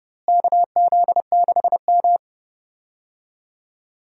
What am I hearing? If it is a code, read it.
Morse code, heard as a keyed tone, about 31 wpm, 700 Hz. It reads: KZ6M
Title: USA Call Signs - 4 Characters - Repeated 2x - 30wpm